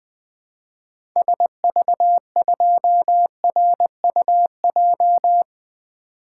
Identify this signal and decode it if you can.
Morse code, heard as a keyed tone, about 21 wpm, 700 Hz. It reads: SV2RUJ
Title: 2023 SOTA Activator Call Signs - 20wpm